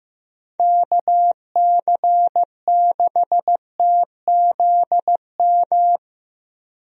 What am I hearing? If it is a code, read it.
Morse code, heard as a keyed tone, about 15 wpm, 700 Hz. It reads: KC6TZM
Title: USA Call Signs - 6 Characters - Repeated 2x - 15wpm